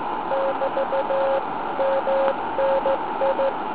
Morse code, heard as a keyed tone, about 16 wpm, 540 Hz, with noise. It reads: <BT> MNI